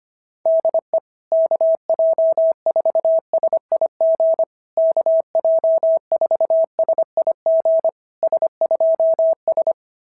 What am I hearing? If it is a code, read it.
Morse code, heard as a keyed tone, about 24 wpm, 650 Hz. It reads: DE KJ4SIG KJ4SIG S2S
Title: SOTA QSO Practice 25wpm